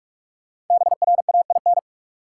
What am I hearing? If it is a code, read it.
Morse code, heard as a keyed tone, about 46 wpm, 700 Hz. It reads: BRAIN
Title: Sentences from Top 1100 Words - New Word Drill Alt - 45wpm